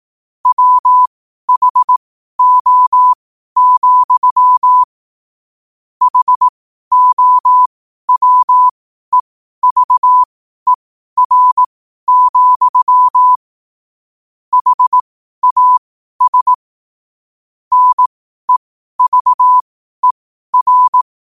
Morse code, heard as a keyed tone, about 18 wpm, spaced as for 12 wpm, 1000 Hz. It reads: WHO, HOWEVER, HAS NEVER
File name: qotdfiles / 12wpm